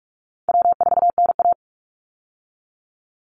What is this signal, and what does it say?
Morse code, heard as a keyed tone, about 44 wpm, 700 Hz. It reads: W4NA
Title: Call Signs Repeated 3x - 45wpm